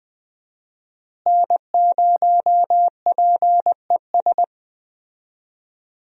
Morse code, heard as a keyed tone, about 20 wpm, 700 Hz. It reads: N0PES
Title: USA Call Signs - 5 Characters - Repeated 3x - 20wpm